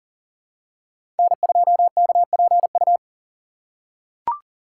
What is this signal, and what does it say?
Morse code, heard as a keyed tone, about 40 wpm, 700 Hz. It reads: N2KPU
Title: USA Call Signs - 2x Character Spacing - 40wpm